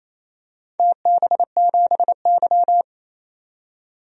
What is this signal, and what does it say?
Morse code, heard as a keyed tone, about 27 wpm, 700 Hz. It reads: TB7Y
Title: DXCC-100 Call Signs - 4 Characters - 28wpm